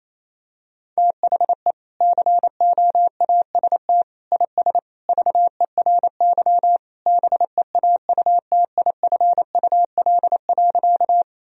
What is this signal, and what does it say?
Morse code, heard as a keyed tone, about 28 wpm, 700 Hz. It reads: THE COAST IS VERY BEAUTIFUL.
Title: Sentences from Top 1000 Words - Review Part 4 - 28wpm